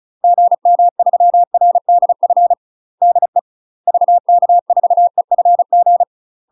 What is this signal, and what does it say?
Morse code, heard as a keyed tone, about 36 wpm, 700 Hz. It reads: GM3RDF DE VK4EFG
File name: UK MASTER QSOs TRAINING / 35WPM / UK MASTER QSOs 0017